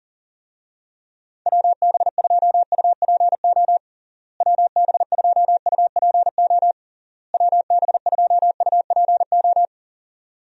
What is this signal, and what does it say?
Morse code, heard as a keyed tone, about 40 wpm, 700 Hz. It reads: WB2UPO WB2UPO WB2UPO
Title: USA Call Signs - 6 Characters - Repeated 3x - 40wpm